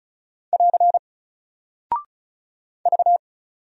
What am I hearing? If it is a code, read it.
Morse code, heard as a keyed tone, about 36 wpm, 700 Hz. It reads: <AR> V
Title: Single Character (Letter:Number:Punctuation:Prosign) Alt 35wpm